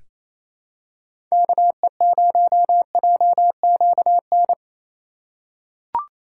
Morse code, heard as a keyed tone, about 28 wpm, 700 Hz. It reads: KE0JQN
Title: USA Call Signs - 6 Characters - 4x Character Spacing - 28wpm